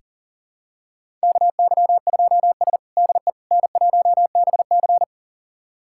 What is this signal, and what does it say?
Morse code, heard as a keyed tone, about 40 wpm, 700 Hz. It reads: KY2S DE N1BC
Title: USA Callsign DE Callsign - 4 Characters - 40wpm